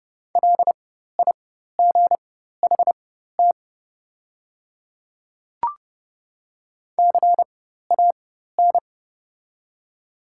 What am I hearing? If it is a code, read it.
Morse code, heard as a keyed tone, about 30 wpm, spaced as for 14 wpm, 700 Hz. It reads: LIGHT CAN
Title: 4x Character Spacing - Top 300 Words - 30wpm